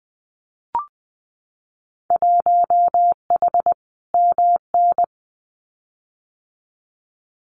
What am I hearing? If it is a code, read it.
Morse code, heard as a keyed tone, about 20 wpm, 700 Hz. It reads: 1H MN
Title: Winter Field Day Exchanges 20wpm